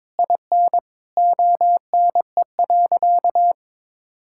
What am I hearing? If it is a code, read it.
Morse code, heard as a keyed tone, about 22 wpm, 700 Hz. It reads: IN ONE.
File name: All-Age Appropriate Jokes - 22wpm